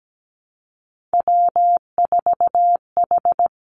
Morse code, heard as a keyed tone, about 17 wpm, 700 Hz. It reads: W4H
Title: Three Challenging Letter-Number Combinations 17wpm